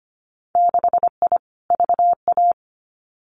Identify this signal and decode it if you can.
Morse code, heard as a keyed tone, about 25 wpm, 700 Hz. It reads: 6I VA
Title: Winter Field Day Exchanges Encore 25wpm